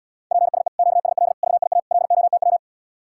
Morse code, heard as a keyed tone, about 80 wpm, 700 Hz. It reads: PSE QSL VIA LOTW